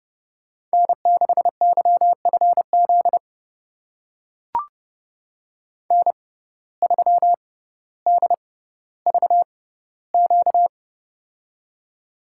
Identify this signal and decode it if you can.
Morse code, heard as a keyed tone, about 30 wpm, spaced as for 15 wpm, 700 Hz. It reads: N6YFZ N3DVQ
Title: USA Call Signs - 5 Characters - 6x Character Spacing - 30wpm